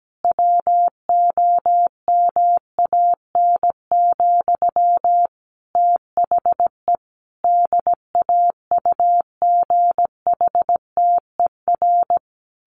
Morse code, heard as a keyed tone, about 17 wpm, 700 Hz. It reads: WOMAN, THE DAUGHTER
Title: Sentences from Top 1500 Words 17wpm